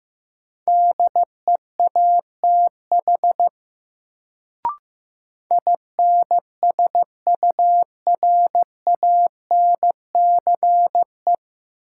Morse code, heard as a keyed tone, about 15 wpm, 700 Hz. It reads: DEATH INSURANCE
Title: Sentences from Top 1100 Words - New Word Drill Alt - 15wpm